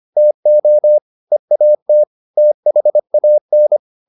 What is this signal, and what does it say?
Morse code, heard as a keyed tone, about 25 wpm, 600 Hz. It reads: TO EAT THAN